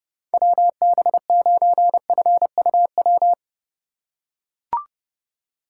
Morse code, heard as a keyed tone, about 30 wpm, 700 Hz. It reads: WB9FUW